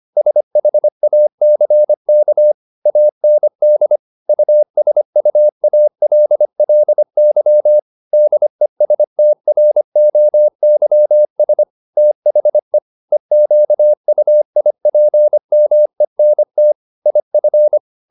Morse code, heard as a keyed tone, about 25 wpm, 600 Hz. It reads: SHACK AND USUALLY DESTROYS THE EQUIPMENT IF